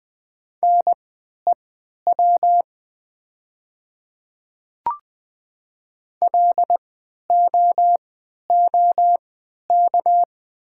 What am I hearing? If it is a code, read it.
Morse code, heard as a keyed tone, about 20 wpm, spaced as for 11 wpm, 700 Hz. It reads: NEW LOOK
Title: 3x Character Spacing - Top 300 Words - 20wpm